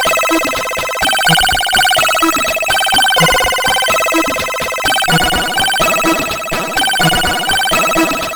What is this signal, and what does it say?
There is no code.